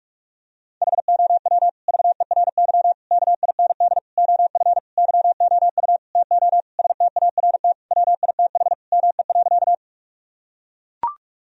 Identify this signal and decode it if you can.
Morse code, heard as a keyed tone, about 41 wpm, 700 Hz. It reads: HOW VERY KIND OF YOU TO START WITH ME.
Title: Sentences from Top 1000 Words - Review Part 1 - 45wpm